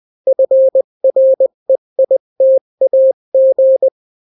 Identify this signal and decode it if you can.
Morse code, heard as a keyed tone, about 20 wpm, 530 Hz. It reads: FREITAG
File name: daily / 18wpm